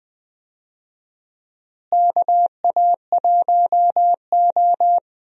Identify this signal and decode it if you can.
Morse code, heard as a keyed tone, about 20 wpm, 700 Hz. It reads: KA1O
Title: USA Call Signs - 4 Characters - Repeated 3x - 20wpm